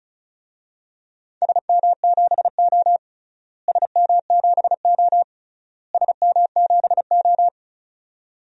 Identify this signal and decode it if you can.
Morse code, heard as a keyed tone, about 34 wpm, 700 Hz. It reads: SM7O SM7O SM7O
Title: DXCC-100 Call Signs - 4 Characters - Repeated 3x - 35wpm